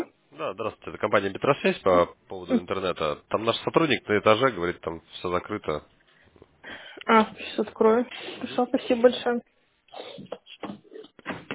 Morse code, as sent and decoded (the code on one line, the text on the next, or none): none